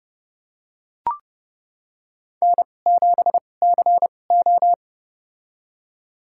Morse code, heard as a keyed tone, about 30 wpm, spaced as for 22 wpm, 700 Hz. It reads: N7CO
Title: USA Call Signs - 4 Characters - 2x Character Spacing - 30wpm